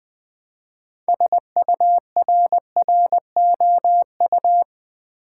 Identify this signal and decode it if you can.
Morse code, heard as a keyed tone, about 20 wpm, 700 Hz. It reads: SURROU
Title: Build A Word - 3 Letters At A Time - 8-Letter Words - 20wpm